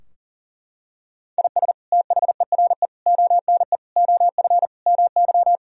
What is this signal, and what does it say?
Morse code, heard as a keyed tone, about 40 wpm, 700 Hz. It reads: IS THERE ONE OF MY
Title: Sentences from Top 1000 Words - Review Part 2 - 40wpm